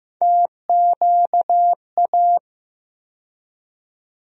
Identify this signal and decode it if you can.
Morse code, heard as a keyed tone, about 15 wpm, 700 Hz. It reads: TQA
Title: USA Call Sign Postfixes - 3 Letters 15wpm